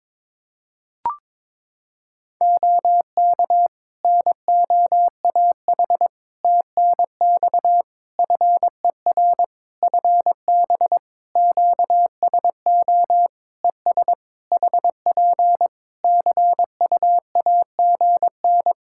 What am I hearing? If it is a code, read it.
Morse code, heard as a keyed tone, about 22 wpm, 700 Hz. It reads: OK NOAH TNX FER FB QSO ES HP CUAGN
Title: Practice QSO - Exchange 4 - Ending - 22wpm